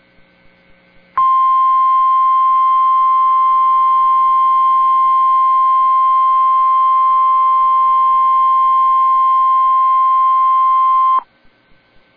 Emergency Alert System (SAME) tones (the 1050 Hz attention tone). It is heard from 1.1 to 11.3 s.